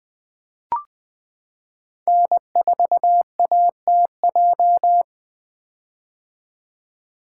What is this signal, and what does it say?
Morse code, heard as a keyed tone, about 20 wpm, 700 Hz. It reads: N4ATJ
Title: 2023 POTA Activator Call Signs - 20wpm